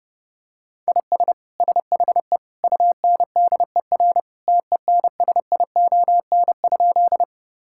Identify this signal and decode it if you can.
Morse code, heard as a keyed tone, about 32 wpm, 700 Hz. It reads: IS SHE UNDER TENSION?
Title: Sentences from Top 1600 Words 30wpm